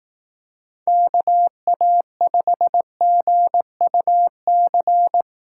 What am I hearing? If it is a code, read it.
Morse code, heard as a keyed tone, about 18 wpm, 700 Hz. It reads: KA5GUC
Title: Speed Racing - USA 2x3 Call Signs 15wpm